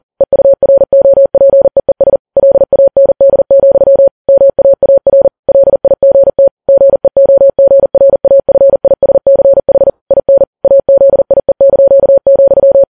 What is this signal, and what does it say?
Morse code, heard as a keyed tone, about 40 wpm, 550 Hz. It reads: EUROPEES LAND, MAAR LIGT GEOGRAFISCH IN AZIE),